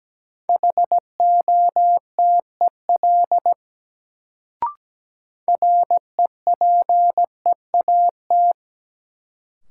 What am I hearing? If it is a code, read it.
Morse code, heard as a keyed tone, about 17 wpm, 700 Hz. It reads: HOTEL REPEAT
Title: Sentences from Top 1600 Words - New Word Drill - 17wpm